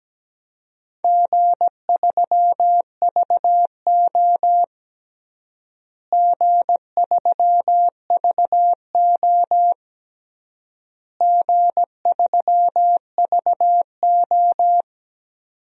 Morse code, heard as a keyed tone, about 17 wpm, 700 Hz. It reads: G3VO G3VO G3VO
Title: DXCC-100 Call Signs - 4 Characters - Repeated 3x - 17wpm